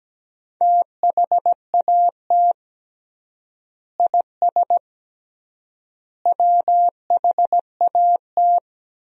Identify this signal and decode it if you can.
Morse code, heard as a keyed tone, about 17 wpm, 700 Hz. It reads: THAT IS WHAT
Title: 3x Word-Spacing - Sentences from Top 100 Words - 17wpm